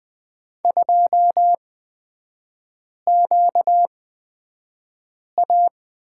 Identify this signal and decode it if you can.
Morse code, heard as a keyed tone, about 20 wpm, spaced as for 5 wpm, 690 Hz. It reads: 2QA
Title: 236 Lesson 38e - Letter q - Three Characters 20wpm